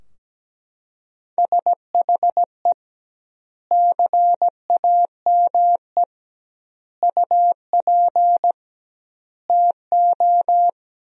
Morse code, heard as a keyed tone, about 17 wpm, 700 Hz. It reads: SHE CAME UP TO
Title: 2x Word-Spacing - Sentences from Top 200 Words - 17wpm